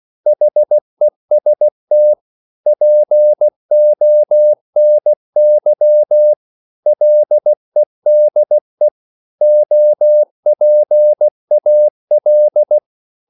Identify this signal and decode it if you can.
Morse code, heard as a keyed tone, about 16 wpm, 600 Hz. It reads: HEST PONY LEDE OPAL
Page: Korte danske ord 16wpm